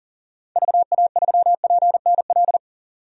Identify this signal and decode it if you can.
Morse code, heard as a keyed tone, about 40 wpm, 700 Hz. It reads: VA3PNL
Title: DXCC-100 Call Signs - 6 Characters - 40wpm